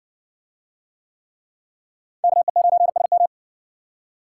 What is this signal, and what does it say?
Morse code, heard as a keyed tone, about 60 wpm, 700 Hz. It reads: KE9SM